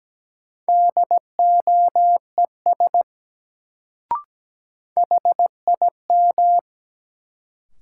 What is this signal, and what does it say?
Morse code, heard as a keyed tone, about 17 wpm, 700 Hz. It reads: DOES HIM
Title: Sets of 2 Words - New Word Drill - 17wpm